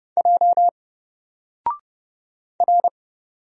Morse code, heard as a keyed tone, about 30 wpm, 700 Hz. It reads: J R
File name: Single-Letter-Number-30wpm